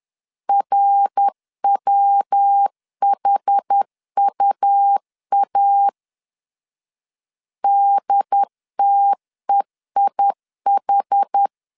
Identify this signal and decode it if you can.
Morse code, heard as a keyed tone, about 10 wpm, 800 Hz. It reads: RWHUA DTEIH
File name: CW / DET / 1 3 1500C S C 10wpm 3335